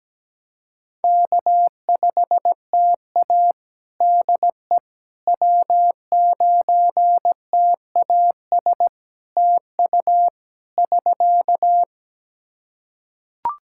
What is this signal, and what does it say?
Morse code, heard as a keyed tone, about 17 wpm, 700 Hz. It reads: K5TA DE W9TAS TU <SK>
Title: Practice QSO - Exchange 4 - Ending - 17wpm